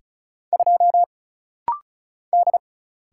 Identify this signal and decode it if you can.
Morse code, heard as a keyed tone, about 36 wpm, 700 Hz. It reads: 2 D
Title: Single Letter-Number - Rapid-Fire 35wpm